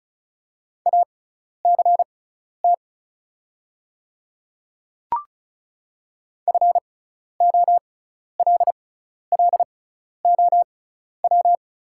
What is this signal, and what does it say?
Morse code, heard as a keyed tone, about 34 wpm, spaced as for 12 wpm, 700 Hz. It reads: ACT FOLLOW
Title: 6x Character Spacing - Top 200 Words - 35wpm